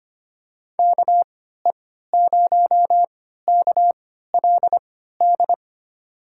Morse code, heard as a keyed tone, about 25 wpm, spaced as for 14 wpm, 700 Hz. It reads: KE0KLD
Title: USA Call Signs - 3x Character Spacing - 25wpm